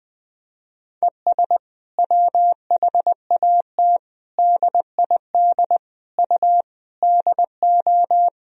Morse code, heard as a keyed tone, about 21 wpm, 700 Hz. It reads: ES WHAT DID U DO
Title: Sentences from Top 300 Words and Abbreviations 20wpm